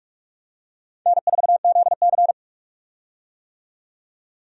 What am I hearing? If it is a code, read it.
Morse code, heard as a keyed tone, about 46 wpm, 700 Hz. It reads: N4ZC
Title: USA Call Signs - 4 Characters - 45wpm